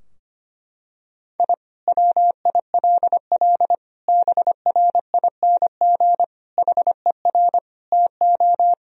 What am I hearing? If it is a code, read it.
Morse code, heard as a keyed tone, about 25 wpm, 700 Hz. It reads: I WILL BRING HER TO
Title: Sentences from Top 1000 Words - Review Part 3 - 25wpm